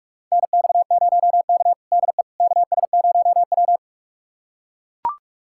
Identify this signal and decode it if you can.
Morse code, heard as a keyed tone, about 44 wpm, 690 Hz. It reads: NX0K DE KS0W
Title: USA Callsign DE Callsign - 4 Characters - 45wpm